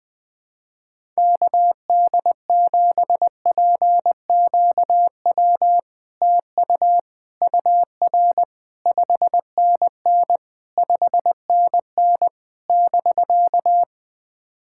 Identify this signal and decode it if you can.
Morse code, heard as a keyed tone, about 20 wpm, 700 Hz. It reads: KD7PQW TU UR 5NN 5NN <BK>
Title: SOTA QSO Practice 20wpm